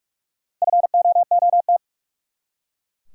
Morse code, heard as a keyed tone, about 42 wpm, 700 Hz. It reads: FOOT
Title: Four Letter Words - 45wpm